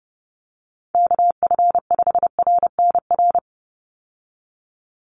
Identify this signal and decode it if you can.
Morse code, heard as a keyed tone, about 30 wpm, 700 Hz. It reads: KF5RNR
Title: USA Call Signs 30wpm